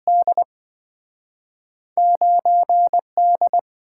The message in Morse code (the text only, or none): D9 D